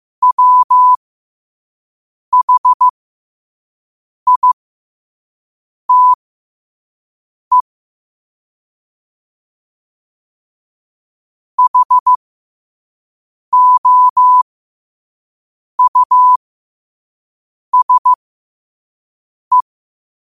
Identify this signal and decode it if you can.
Morse code, heard as a keyed tone, about 15 wpm, spaced as for 5 wpm, 1000 Hz. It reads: WHITE HOUSE
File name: qotdfiles / 5wpm